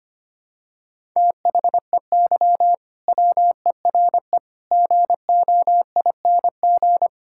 Morse code, heard as a keyed tone, about 25 wpm, 700 Hz. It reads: THEY WERE GOING
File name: practice / 25wpm / Sentences with 3-Word Phrases - 25wpm